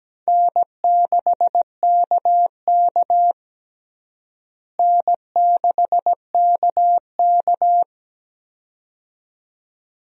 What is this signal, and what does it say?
Morse code, heard as a keyed tone, about 17 wpm, 700 Hz. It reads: N6KK N6KK
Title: USA Call Signs - 4 Characters - Repeated 2x - 17wpm